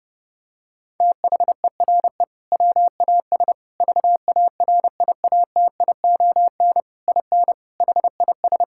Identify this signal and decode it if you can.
Morse code, heard as a keyed tone, about 32 wpm, 700 Hz. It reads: THERE WAS VARIATION IN HIS
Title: Sentences from Top 1600 Words 30wpm